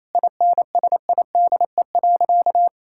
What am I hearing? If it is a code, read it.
Morse code, heard as a keyed tone, about 28 wpm, 700 Hz. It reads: INSIDE.
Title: Sentences from Top 1000 Words - Review Part 6 - 28wpm